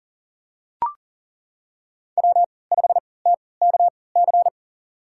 Morse code, heard as a keyed tone, about 40 wpm, spaced as for 23 wpm, 700 Hz. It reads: W5TKC
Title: USA Call Signs - 5 Characters - 3x Character Spacing - 40wpm